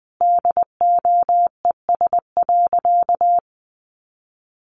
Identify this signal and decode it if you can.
Morse code, heard as a keyed tone, about 20 wpm, 700 Hz. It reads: DOES.
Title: Sentences from Top 900 Words 20wpm